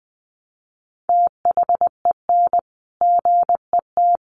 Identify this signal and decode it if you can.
Morse code, heard as a keyed tone, about 19 wpm, 700 Hz. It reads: THEN GET
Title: Sentences from Top 400 Words 20wpm